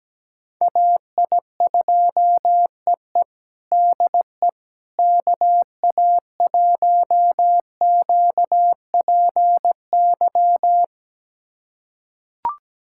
Morse code, heard as a keyed tone, about 17 wpm, 700 Hz. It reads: AI2EE DE KA1QPY
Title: USA Callsign DE Callsign 17wpm